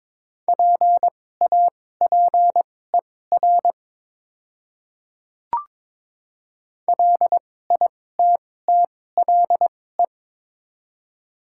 Morse code, heard as a keyed tone, about 22 wpm, spaced as for 16 wpm, 700 Hz. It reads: PAPER LITTLE
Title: 2x Character Spacing - Top 300 Words - 22wpm